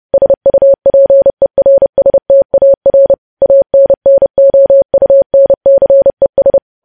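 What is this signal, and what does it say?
Morse code, heard as a keyed tone, about 30 wpm, 550 Hz. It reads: SUPERSTAR ANNOUNCES
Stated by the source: Morse code sending of news headlines at 30 wpm.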